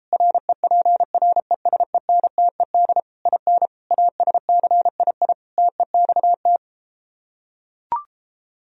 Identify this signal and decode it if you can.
Morse code, heard as a keyed tone, about 33 wpm, 700 Hz. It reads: REPRESENTED IN ASCII TEXT